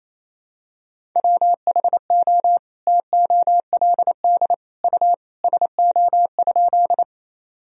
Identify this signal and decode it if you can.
Morse code, heard as a keyed tone, about 28 wpm, 700 Hz. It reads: WHO TOLD U SO?
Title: Sentences from Top 400 Words and Abbreviations 28wpm